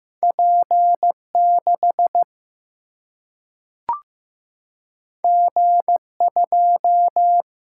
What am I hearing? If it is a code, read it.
Morse code, heard as a keyed tone, about 15 wpm, 700 Hz. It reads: P6 G2
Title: Single-Letter Single-Number 15wpm